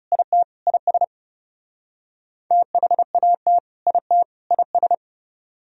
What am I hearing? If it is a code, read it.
Morse code, heard as a keyed tone, about 37 wpm, 700 Hz. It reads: IT IS THAT IT IS
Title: Speed-Racing - Sets of 3 Words 30wpm